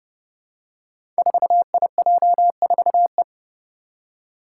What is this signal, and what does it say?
Morse code, heard as a keyed tone, about 30 wpm, 700 Hz. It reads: 4IJ4E
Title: Five Challenging Letter-Number Combinations 30wpm